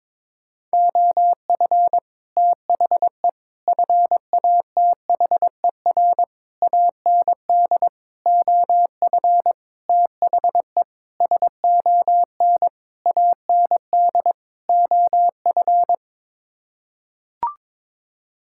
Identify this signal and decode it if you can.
Morse code, heard as a keyed tone, about 22 wpm, 700 Hz. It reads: OF THE FATHER AND OF THE SON AND OF
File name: Sets of 9 Words - 22wpm